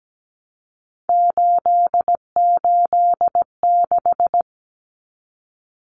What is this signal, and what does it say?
Morse code, heard as a keyed tone, about 17 wpm, 700 Hz. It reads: 886
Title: Numbers - 3-Digit 17wpm